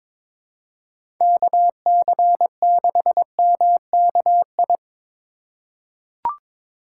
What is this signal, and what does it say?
Morse code, heard as a keyed tone, about 21 wpm, 700 Hz. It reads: KC6MKI
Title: USA Call Signs - 6 Characters - 2x Character Spacing - 22wpm